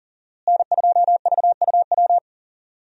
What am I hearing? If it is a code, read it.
Morse code, heard as a keyed tone, about 37 wpm, 700 Hz. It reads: N2VUW
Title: USA Call Signs - 5 Characters - Repeated 3x - 40wpm